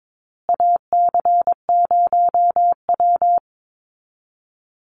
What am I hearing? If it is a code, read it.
Morse code, heard as a keyed tone, about 22 wpm, 700 Hz. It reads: AC0W
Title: Call Signs Repeated 2x - 22wpm